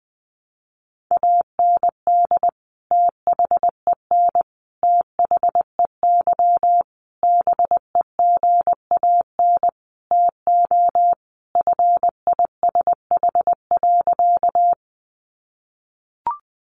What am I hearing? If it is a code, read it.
Morse code, heard as a keyed tone, about 20 wpm, 700 Hz. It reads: AND THEN THEY BEGAN TO FISH.
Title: Sentences from Top 300 Words 20wpm